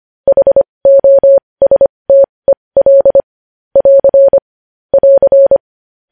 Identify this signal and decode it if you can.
Morse code, heard as a keyed tone, about 25 wpm, spaced as for 20 wpm, 550 Hz. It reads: HOSTEL <AR> <AR>
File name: podcast / News headlines-1x-25x20